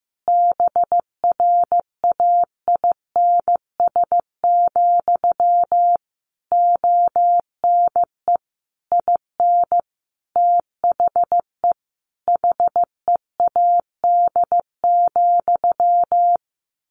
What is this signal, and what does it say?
Morse code, heard as a keyed tone, about 15 wpm, 700 Hz. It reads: BRAINS, ONE IN THE HEAD,
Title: A Sound Of Thunder 15wpm